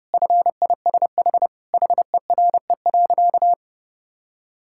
Morse code, heard as a keyed tone, about 30 wpm, 700 Hz. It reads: FISH HERE.
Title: Sentences from Top 1000 Words - Review Part 6 - 30wpm